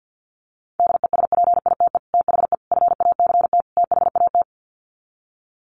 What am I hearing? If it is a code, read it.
Morse code, heard as a keyed tone, about 48 wpm, 700 Hz. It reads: DESPITE THE FACT THAT
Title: Sets of 4 Words 50wpm